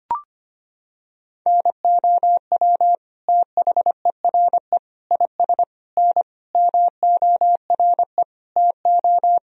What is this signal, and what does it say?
Morse code, heard as a keyed tone, about 25 wpm, 700 Hz. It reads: NOW THERE IS N MORE TO
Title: Sentences from Top 100 Words and Abbreviations 25wpm